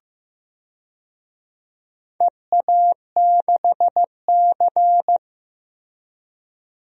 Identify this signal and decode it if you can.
Morse code, heard as a keyed tone, about 15 wpm, 700 Hz. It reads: EA6C
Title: DXCC-100 Call Signs - 4 Characters - Repeated 2x - 15wpm